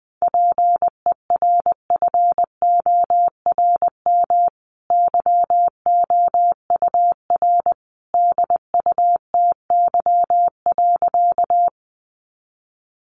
Morse code, heard as a keyed tone, about 20 wpm, 700 Hz. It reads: PERFORM YOUR DUTY.
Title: Sentences from Top 1500 Words 20wpm